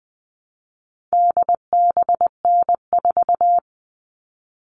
Morse code, heard as a keyed tone, about 20 wpm, 700 Hz. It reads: DBN4
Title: All Cut Numbers - 4-Digit 20wpm